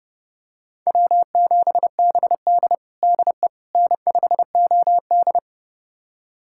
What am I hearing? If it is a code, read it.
Morse code, heard as a keyed tone, about 30 wpm, 700 Hz. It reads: W7BD DE N5OD
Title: USA Callsign DE Callsign - 4 Characters - 30wpm